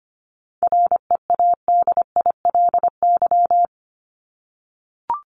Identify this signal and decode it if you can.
Morse code, heard as a keyed tone, about 25 wpm, 700 Hz. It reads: READILY
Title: LY Suffix - Top 100 Words with Base Words 25wpm